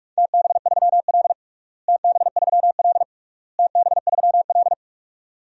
Morse code, heard as a keyed tone, about 46 wpm, 700 Hz. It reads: TB3L TB3L TB3L
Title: DXCC-100 Call Signs - 4 Characters - Repeated 3x - 45wpm